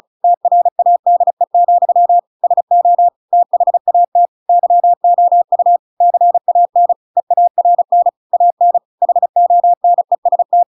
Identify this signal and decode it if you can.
Morse code, heard as a keyed tone, about 34 wpm, 700 Hz. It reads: TRADE, SO THAT YOU CAN EARN AN HONEST